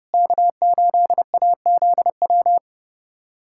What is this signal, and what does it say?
Morse code, heard as a keyed tone, about 32 wpm, 700 Hz. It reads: K8AZW
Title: USA Callsign DE Callsign - 5 Characters - 30wpm